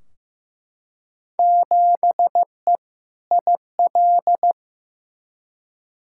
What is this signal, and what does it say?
Morse code, heard as a keyed tone, about 15 wpm, 700 Hz. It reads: 7E IL
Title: Field Day Exchanges Encore - 15wpm